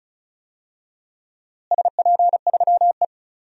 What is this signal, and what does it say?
Morse code, heard as a keyed tone, about 36 wpm, 700 Hz. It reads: SP3E
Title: DXCC-100 Call Signs - 4 Characters - Repeated 3x - 35wpm